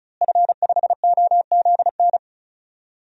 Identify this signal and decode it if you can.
Morse code, heard as a keyed tone, about 34 wpm, 700 Hz. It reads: F5OZN
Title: DXCC-100 Call Signs - 5 Characters - Repeated 3x - 35wpm